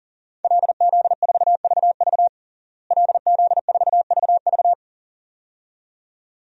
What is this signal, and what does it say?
Morse code, heard as a keyed tone, about 40 wpm, 700 Hz. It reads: LZ4VV LZ4VV
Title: DXCC-100 Call Signs - 5 Characters - Repeated 2x - 40wpm